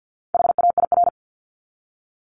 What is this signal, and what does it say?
Morse code, heard as a keyed tone, about 50 wpm, 700 Hz. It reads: HAIR
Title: Top 1751-2000 Words 50wpm